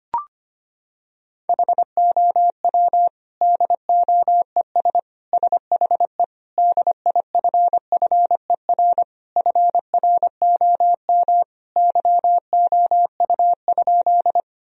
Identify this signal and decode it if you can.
Morse code, heard as a keyed tone, about 25 wpm, 700 Hz. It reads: HOW DOES SHE DIFFER FROM YOU?
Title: Sentences from Top 1000 Words - Review Part 1 - 25wpm